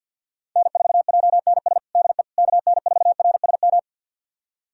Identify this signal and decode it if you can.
Morse code, heard as a keyed tone, about 50 wpm, 690 Hz. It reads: N4JNS DE KN4RSM
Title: USA Callsign DE Callsign 50wpm